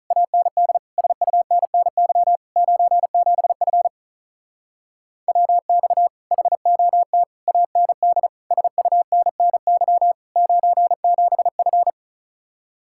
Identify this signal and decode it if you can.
Morse code, heard as a keyed tone, about 44 wpm, 700 Hz. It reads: AND SUNNY 97F WX HOT AND SUNNY 97F
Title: Speed-Racing Common QSO Phrases Encore - 35wpm